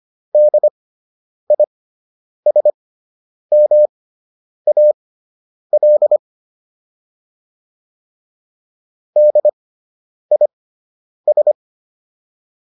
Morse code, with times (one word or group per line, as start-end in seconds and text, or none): 0.3-6.2 DISMAL
9.2-11.5 DIS